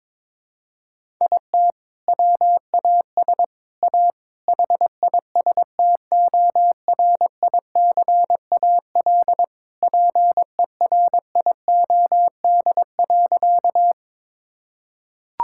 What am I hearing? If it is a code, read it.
Morse code, heard as a keyed tone, about 22 wpm, 700 Hz. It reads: IT WAS A HISTORICAL PERIOD.
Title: Sentences from Top 1000 Words - Review Part 5 - 22wpm